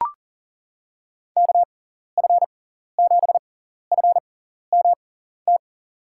Morse code, heard as a keyed tone, about 37 wpm, spaced as for 14 wpm, 700 Hz. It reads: KF7FMT
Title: USA Call Signs - 6x Character Spacing - 40wpm